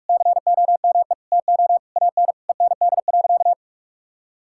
Morse code, heard as a keyed tone, about 45 wpm, 700 Hz. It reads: COME TO AN END.
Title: Sentences from Top 1000 Words - Review Part 5 - 45wpm